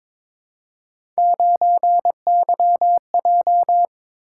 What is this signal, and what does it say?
Morse code, heard as a keyed tone, about 22 wpm, 700 Hz. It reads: 9YJ
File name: practice / 22wpm / Three Letter-Number Combinations - 22wpm